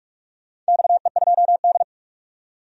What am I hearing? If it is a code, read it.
Morse code, heard as a keyed tone, about 44 wpm, 700 Hz. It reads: XE2D
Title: DXCC-100 Call Signs - 4 Characters - 45wpm